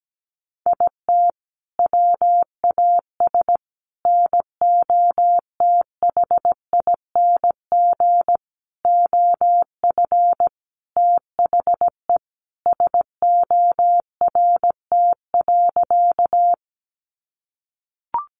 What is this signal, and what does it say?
Morse code, heard as a keyed tone, about 17 wpm, 700 Hz. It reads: IT WAS NOTHING OF THE SORT.
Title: Sentences from Top 1300 Words 17wpm